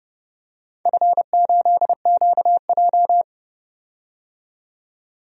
Morse code, heard as a keyed tone, about 30 wpm, 700 Hz. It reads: F8QJ
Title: DXCC-100 Call Signs - 4 Characters - 30wpm